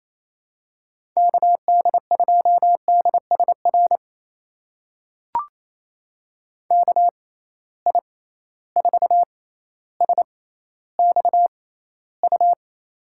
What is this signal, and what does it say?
Morse code, heard as a keyed tone, about 28 wpm, spaced as for 14 wpm, 700 Hz. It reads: KD2DSR KI4SXU
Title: USA Call Signs - 6x Character Spacing - 28wpm